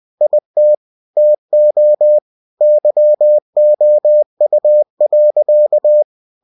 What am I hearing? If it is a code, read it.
Morse code, heard as a keyed tone, about 20 wpm, 600 Hz. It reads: IT TO YOU.
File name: cw-books / winnie-the-pooh / winnie-20wpm-page-0098